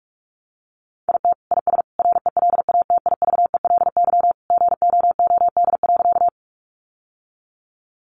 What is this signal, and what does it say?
Morse code, heard as a keyed tone, about 44 wpm, 700 Hz. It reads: IT IS RELATIVELY GOOD.
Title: Sentences from Top 1100 Words 45wpm